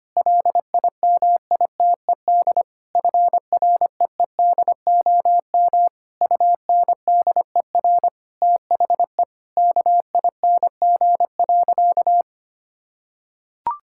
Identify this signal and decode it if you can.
Morse code, heard as a keyed tone, about 25 wpm, 700 Hz. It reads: LIMITED FREEDOM UNDER THE KING.
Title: Sentences from Top 1000 Words - Review Part 6 - 25wpm